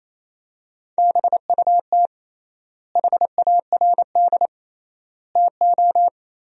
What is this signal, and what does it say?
Morse code, heard as a keyed tone, about 28 wpm, 700 Hz. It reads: BUT HARD TO
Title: 3x Word-Spacing - Sentences from Top 300 Words - 28wpm